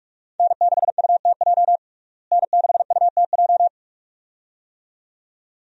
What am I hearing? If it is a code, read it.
Morse code, heard as a keyed tone, about 47 wpm, 700 Hz. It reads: N6UTJ N6UTJ
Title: USA Call Signs - 5 Characters - Repeated 2x - 45wpm